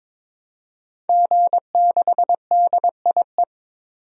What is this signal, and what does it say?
Morse code, heard as a keyed tone, about 22 wpm, 700 Hz. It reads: G6DIE
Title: DXCC-100 Call Signs - 5 Characters - 22wpm